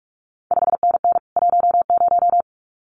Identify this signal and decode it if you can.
Morse code, heard as a keyed tone, about 46 wpm, 700 Hz. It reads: 5NN 10
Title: CQ WW DX Contest Exchange Encore - 45wpm